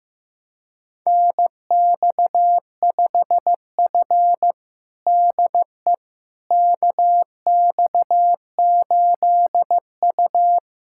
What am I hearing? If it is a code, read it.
Morse code, heard as a keyed tone, about 16 wpm, 700 Hz. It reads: NX5F DE KX8U
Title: USA Callsign DE Callsign - 4 Characters - 15wpm